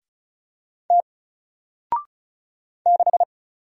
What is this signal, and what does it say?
Morse code, heard as a keyed tone, about 36 wpm, 700 Hz. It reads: T 6
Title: Single Letter-Number Alt 35wpm